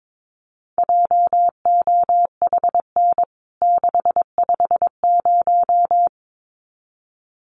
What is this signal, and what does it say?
Morse code, heard as a keyed tone, about 22 wpm, 700 Hz. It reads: JOHN 650
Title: CWT Exchange 22wpm